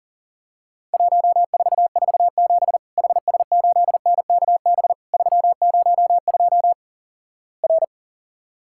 Morse code, heard as a keyed tone, about 42 wpm, 700 Hz. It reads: 1447 HS8NKB 302 R
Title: Worked All Europe DX Contest - QTC Exchange - 40wpm